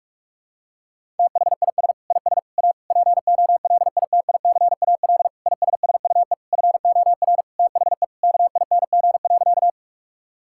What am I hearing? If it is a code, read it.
Morse code, heard as a keyed tone, about 46 wpm, 700 Hz. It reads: THIS IS A POLITICAL ISSUE FOR THE KING.
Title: Sentences from Top 1000 Words - Review Part 6 - 45wpm